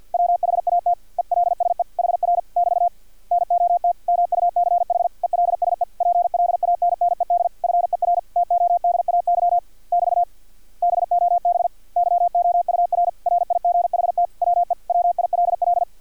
Morse code, heard as a keyed tone, about 50 wpm, 700 Hz. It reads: WHAT ELSE HR <BT> NOT MUCH ELSE PLANNED FER TODAY <BT> BOB YOUR RIGHT WE WILL
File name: archived / CRSnet